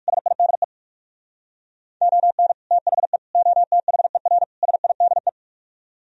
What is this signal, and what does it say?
Morse code, heard as a keyed tone, about 58 wpm, 700 Hz. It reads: SIDE ON THE OTHER SIDE